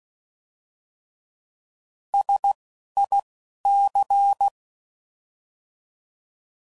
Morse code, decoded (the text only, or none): SIC